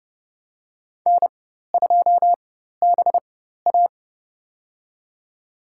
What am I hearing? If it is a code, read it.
Morse code, heard as a keyed tone, about 30 wpm, spaced as for 14 wpm, 700 Hz. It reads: N2BA
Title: USA Call Signs - 4 Characters - 4x Character Spacing - 30wpm